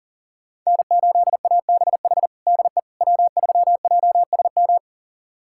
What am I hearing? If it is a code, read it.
Morse code, heard as a keyed tone, about 40 wpm, 700 Hz. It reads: N8ABH DE W3JSM
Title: USA Callsign DE Callsign - 5 Characters - 40wpm